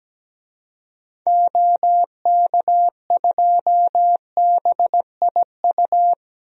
Morse code, heard as a keyed tone, about 17 wpm, 700 Hz. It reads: OK2BIU
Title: DXCC-100 Call Signs - 6 Characters - Repeated 2x - 17wpm